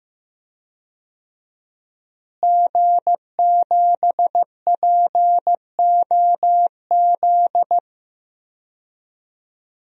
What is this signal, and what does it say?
Morse code, heard as a keyed tone, about 15 wpm, 700 Hz. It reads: G7POZ
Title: DXCC-100 Call Signs - 5 Characters - Repeated 3x - 15wpm